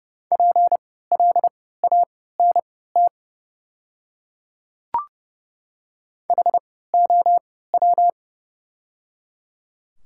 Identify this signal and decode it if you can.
Morse code, heard as a keyed tone, about 30 wpm, spaced as for 17 wpm, 700 Hz. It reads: PLANT HOW
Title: 3x Character Spacing - Top 300 Words - 30wpm